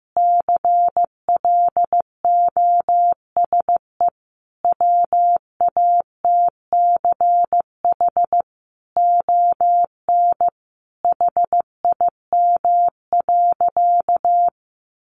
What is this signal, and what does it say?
Morse code, heard as a keyed tone, about 15 wpm, 700 Hz. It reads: CLOSE WATCH ON HIM.
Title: Sentences from Top 1300 Words 15wpm